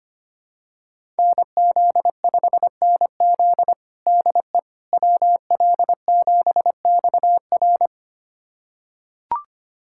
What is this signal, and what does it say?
Morse code, heard as a keyed tone, about 25 wpm, 700 Hz. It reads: NZ5NZ DE WL7XR
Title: USA Callsign DE Callsign - 5 Characters - 25wpm